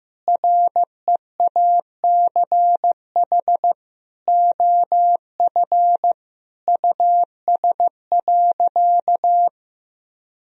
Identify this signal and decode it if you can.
Morse code, heard as a keyed tone, about 15 wpm, 700 Hz. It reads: REACH OF US.
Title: Sentences from Top 1000 Words - Review Part 4 - 15wpm